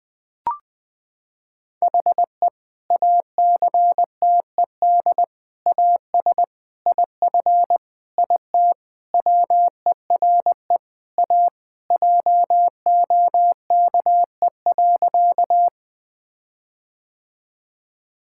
Morse code, heard as a keyed tone, about 21 wpm, 700 Hz. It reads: HE ACTED AS IF IT WERE A JOKE.
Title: Sentences with 5-Word Phrases - 20wpm